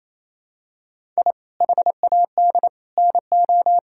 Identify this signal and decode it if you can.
Morse code, heard as a keyed tone, about 28 wpm, 700 Hz. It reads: I HAD NO
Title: Sentences from Top 1000 Words - Review Part 6 - 28wpm